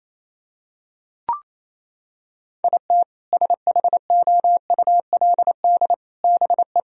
Morse code, heard as a keyed tone, about 28 wpm, 700 Hz. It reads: IT SHOULD BE
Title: Sentences from Top 1000 Words - Review Part 4 - 28wpm